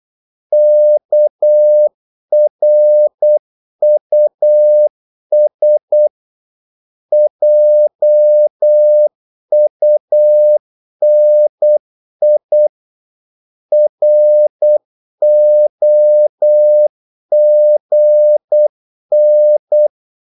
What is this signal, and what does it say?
Morse code, heard as a keyed tone, about 8 wpm, 600 Hz. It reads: KRUS JUNI ROGN
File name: Korte ord DK 8wpm